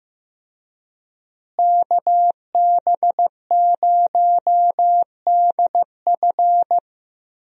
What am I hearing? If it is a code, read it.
Morse code, heard as a keyed tone, about 15 wpm, 700 Hz. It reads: KB0DF
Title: USA Call Signs - 5 Characters - Repeated 2x - 15wpm